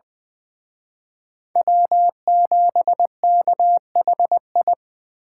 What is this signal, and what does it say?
Morse code, heard as a keyed tone, about 23 wpm, 700 Hz. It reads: W7KHI
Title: Speed Racing - USA 1x3 Call Signs 20wpm